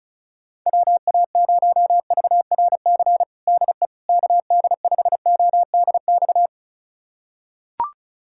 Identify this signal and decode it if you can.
Morse code, heard as a keyed tone, about 34 wpm, 700 Hz. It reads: WA0VRC DE KD5ODX
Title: USA Callsign DE Callsign - 6 Characters - 35wpm